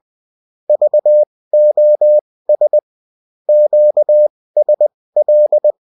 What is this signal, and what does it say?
Morse code, heard as a keyed tone, about 20 wpm, spaced as for 16 wpm, 600 Hz. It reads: VOS QSL